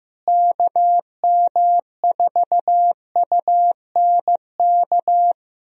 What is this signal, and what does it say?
Morse code, heard as a keyed tone, about 15 wpm, 700 Hz. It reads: KM4UNK
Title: USA Call Signs - 6 Characters - Repeated 2x - 15wpm